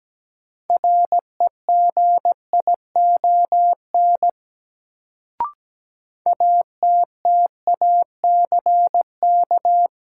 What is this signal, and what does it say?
Morse code, heard as a keyed tone, about 17 wpm, 700 Hz. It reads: REGION ATTACK
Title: Sentences from Top 1500 Words - New Word Drill Alt - 17wpm